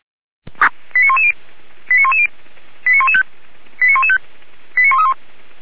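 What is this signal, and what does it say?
Five-tone selective call (ZVEI), heard as quick runs of five tones, tones of about 70 ms. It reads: 89110 89110 89115 89115 89131
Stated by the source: ZVEI five-tone call 89109 received on an SDR.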